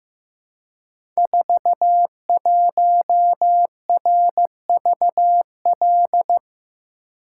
Morse code, heard as a keyed tone, about 15 wpm, 700 Hz. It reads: 41RVL
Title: Code Groups - 5 Letter:Numbers - 15wpm